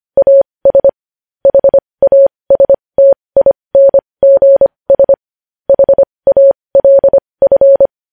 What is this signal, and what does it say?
Morse code, heard as a keyed tone, about 25 wpm, spaced as for 20 wpm, 550 Hz. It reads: AS HASTINGS HALF
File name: podcast / News headlines-1x-25x20